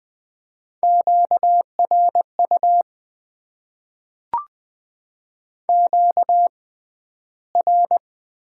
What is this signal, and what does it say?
Morse code, heard as a keyed tone, about 20 wpm, spaced as for 12 wpm, 700 Hz. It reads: QRU QR
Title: Top 90 QSO Elements - 6x Character Spacing - 20wpm